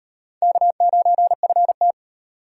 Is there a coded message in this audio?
Yes: Morse code from 0.4 to 1.9 s, heard as a keyed tone, about 38 wpm, 700 Hz. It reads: K9FT